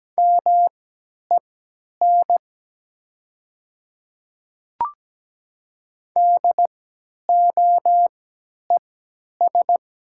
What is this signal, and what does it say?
Morse code, heard as a keyed tone, about 17 wpm, spaced as for 10 wpm, 700 Hz. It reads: MEN DOES
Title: 3x Character Spacing - Top 200 Words - 17wpm